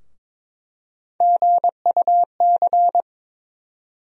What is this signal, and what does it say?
Morse code, heard as a keyed tone, about 22 wpm, 700 Hz. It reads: GUC
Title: USA Call Sign Postfixes - 3 Letters 22wpm